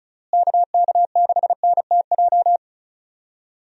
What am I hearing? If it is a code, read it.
Morse code, heard as a keyed tone, about 36 wpm, 700 Hz. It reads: KK6NTJ
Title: USA Call Signs - 6 Characters - Repeated 3x - 35wpm